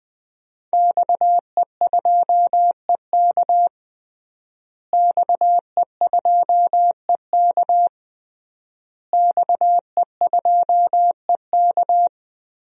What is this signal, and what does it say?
Morse code, heard as a keyed tone, about 20 wpm, 700 Hz. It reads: XE2EK XE2EK XE2EK
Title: DXCC-100 Call Signs - 5 Characters - Repeated 3x - 20wpm